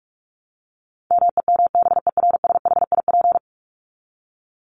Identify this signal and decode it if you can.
Morse code, heard as a keyed tone, about 44 wpm, 700 Hz. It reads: MEMBERSHIP
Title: Top 1251-1500 Words 45wpm